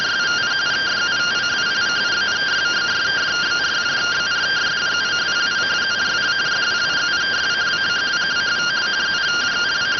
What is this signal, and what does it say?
Two-tone teleprinter signal, about 170 Hz shift.